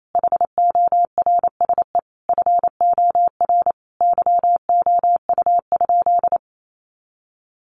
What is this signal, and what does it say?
Morse code, heard as a keyed tone, about 28 wpm, 700 Hz. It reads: HORSE FOR YOU?
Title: Sentences from Top 1400 Words 28wpm